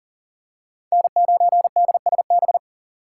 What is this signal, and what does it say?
Morse code, heard as a keyed tone, about 40 wpm, 700 Hz. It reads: N9DSB
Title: USA Call Signs - 5 Characters - 3x Character Spacing - 40wpm